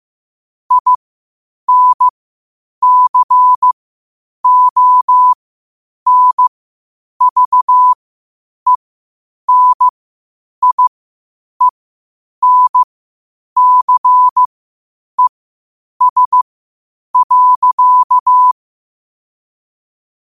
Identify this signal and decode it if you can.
Morse code, heard as a keyed tone, about 15 wpm, spaced as for 8 wpm, 1000 Hz. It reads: INCONVENIENCES.
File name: qotdfiles / 8wpm